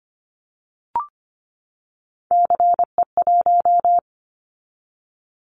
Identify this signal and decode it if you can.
Morse code, heard as a keyed tone, about 25 wpm, 700 Hz. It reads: CE1
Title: Callsign Prefixes for Top 100 DXCC - 25wpm